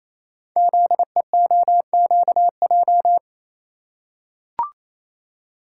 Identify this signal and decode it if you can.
Morse code, heard as a keyed tone, about 27 wpm, 700 Hz. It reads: ZEOQJ
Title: Code Groups - 5 Letters - 28wpm